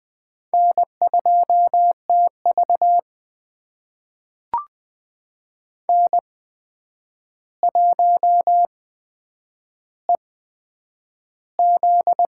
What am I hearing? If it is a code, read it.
Morse code, heard as a keyed tone, about 20 wpm, spaced as for 9 wpm, 700 Hz. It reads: N2TV N1EZ
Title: USA Call Signs - 8x Character Spacing - 20wpm